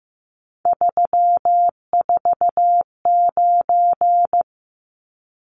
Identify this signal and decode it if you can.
Morse code, heard as a keyed tone, about 15 wpm, 700 Hz. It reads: 349
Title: Numbers - 3-Digit 15wpm